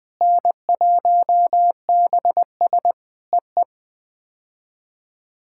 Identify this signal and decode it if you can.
Morse code, heard as a keyed tone, about 20 wpm, 700 Hz. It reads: N1BS EE
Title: POTA QSO Practice - 20wpm